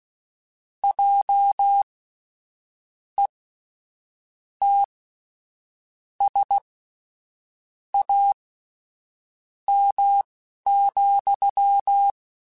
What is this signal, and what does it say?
Morse code, heard as a keyed tone, about 16 wpm, spaced as for 6 wpm, 800 Hz. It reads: JETSAM,
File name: Misc / Misc-mp3s / WordPairs sp @10